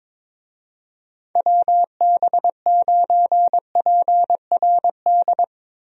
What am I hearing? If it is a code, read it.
Morse code, heard as a keyed tone, about 22 wpm, 700 Hz. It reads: WB9PRD
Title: USA Call Signs - 6 Characters - Repeated 2x - 22wpm